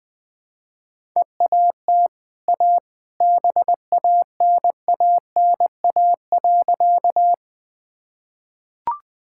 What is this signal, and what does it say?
Morse code, heard as a keyed tone, about 20 wpm, 700 Hz. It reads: EAT A BANANA.
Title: Easy Sentences with Hints - 20wpm